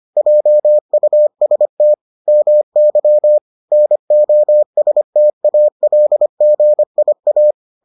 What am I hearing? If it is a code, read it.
Morse code, heard as a keyed tone, about 25 wpm, 600 Hz. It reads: JUST MY NOSTALGIA